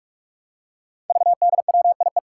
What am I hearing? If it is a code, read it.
Morse code, heard as a keyed tone, about 44 wpm, 700 Hz. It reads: VDWIE
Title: Five Challenging Letter-Number Combinations 45wpm